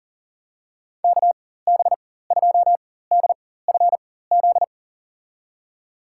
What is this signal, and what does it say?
Morse code, heard as a keyed tone, about 40 wpm, spaced as for 19 wpm, 700 Hz. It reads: KB2DFZ
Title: USA Call Signs - 6 Characters - 4x Character Spacing - 40wpm